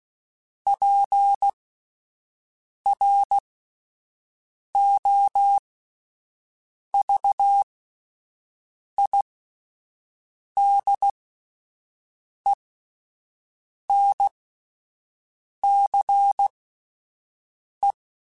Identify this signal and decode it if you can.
Morse code, heard as a keyed tone, about 16 wpm, spaced as for 5 wpm, 800 Hz. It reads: PROVIDENCE